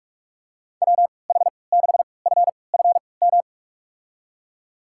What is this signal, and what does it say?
Morse code, heard as a keyed tone, about 45 wpm, spaced as for 25 wpm, 700 Hz. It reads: WH6FFM
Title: USA Call Signs - 3x Character Spacing - 45wpm